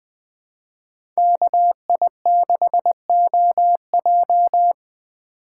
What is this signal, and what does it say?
Morse code, heard as a keyed tone, about 20 wpm, 700 Hz. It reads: KI6OJ
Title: USA Call Signs - 5 Characters - 3x Character Spacing - 20wpm